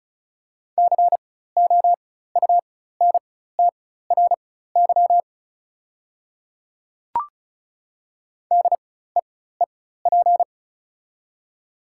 Morse code, heard as a keyed tone, about 36 wpm, spaced as for 16 wpm, 700 Hz. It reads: COUNTRY DEEP
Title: 4x Character Spacing - Top 500 Words - 35wpm